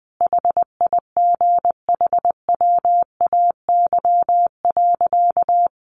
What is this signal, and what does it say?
Morse code, heard as a keyed tone, about 20 wpm, 700 Hz. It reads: HIGHWAY.
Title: Sentences from Top 1400 Words 20wpm